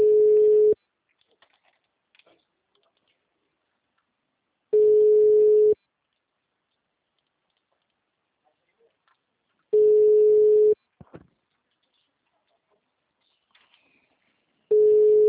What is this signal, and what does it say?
Ringback tone (Europe, 425 Hz), with speech.
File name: out- -2002- - - .43204